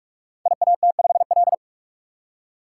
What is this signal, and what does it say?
Morse code, heard as a keyed tone, about 42 wpm, 700 Hz. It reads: IAT5L
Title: Code Groups - 5 Letter:Numbers - 45wpm